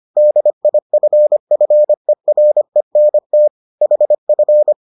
Morse code, heard as a keyed tone, about 25 wpm, 600 Hz. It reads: DIFFERENT HF